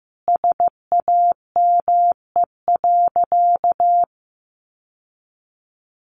Morse code, heard as a keyed tone, about 15 wpm, 700 Hz. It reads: SAME.
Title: Sentences from Top 1000 Words 15wpm